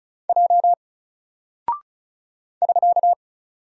Morse code, heard as a keyed tone, about 36 wpm, 700 Hz. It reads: J <SK>
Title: Single Character (Letter:Number:Punctuation:Prosign) Alt 35wpm